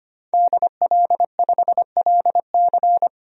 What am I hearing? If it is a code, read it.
Morse code, heard as a keyed tone, about 25 wpm, 700 Hz. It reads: DL5LC
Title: DXCC-100 Call Signs - 5 Characters - 25wpm